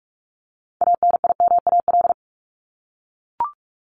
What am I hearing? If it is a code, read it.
Morse code, heard as a keyed tone, about 44 wpm, 700 Hz. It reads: ANIMAL
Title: Top 200 Words 45wpm